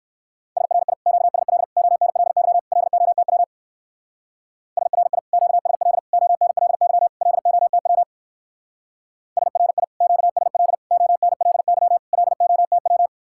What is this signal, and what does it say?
Morse code, heard as a keyed tone, about 54 wpm, 700 Hz. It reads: SRI QSL ONLY LOTW SRI QSL ONLY LOTW SRI QSL ONLY LOTW
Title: Speed-Racing Common QSO Phrases - 45wpm